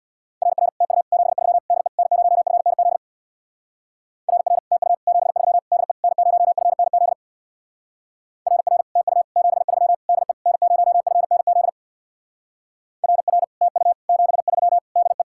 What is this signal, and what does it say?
Morse code, heard as a keyed tone, about 65 wpm, 700 Hz. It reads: RR TU 73 DE N0FNZ RR TU 73 DE N0FNZ RR TU 73 DE N0FNZ RR TU 73 DE